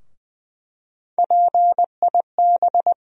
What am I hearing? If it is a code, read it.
Morse code, heard as a keyed tone, about 20 wpm, 700 Hz. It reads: PIB
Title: USA Call Sign Postfixes - 3 Letters 20wpm